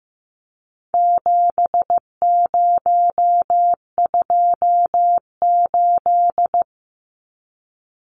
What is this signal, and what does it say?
Morse code, heard as a keyed tone, about 15 wpm, 700 Hz. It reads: 7028
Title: Numbers - 4-Digit 15wpm